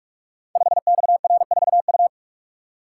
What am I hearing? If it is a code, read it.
Morse code, heard as a keyed tone, about 44 wpm, 700 Hz. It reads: 5XR4U
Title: Code Groups - 5 Letter:Numbers - 45wpm